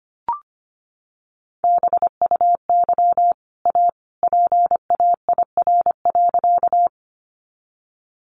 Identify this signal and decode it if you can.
Morse code, heard as a keyed tone, about 25 wpm, 700 Hz. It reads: BUY A PAIR.
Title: Sentences from Top 1200 Words 25wpm